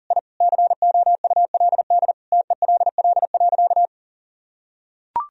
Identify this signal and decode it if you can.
Morse code, heard as a keyed tone, about 40 wpm, 700 Hz. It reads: I COULD TELL.
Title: Sentences from Top 1000 Words - Review Part 4 - 40wpm